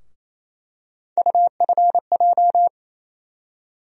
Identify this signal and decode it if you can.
Morse code, heard as a keyed tone, about 28 wpm, 700 Hz. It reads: UFJ
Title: USA Call Sign Postfixes - 3 Letters 28wpm